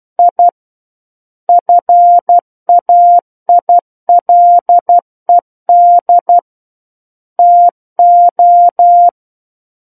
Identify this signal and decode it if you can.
Morse code, heard as a keyed tone, about 12 wpm, 700 Hz. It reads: I FAILED TO
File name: Book One / 03-Chapter 3 - 12 WPM